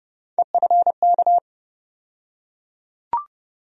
Morse code, heard as a keyed tone, about 30 wpm, 700 Hz. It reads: EFK
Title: USA Call Sign Postfixes - 3 Letters 30wpm